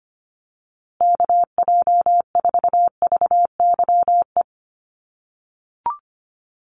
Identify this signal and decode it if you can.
Morse code, heard as a keyed tone, about 25 wpm, 700 Hz. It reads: KJ4VYE
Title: USA Call Signs Repeated 2x 25wpm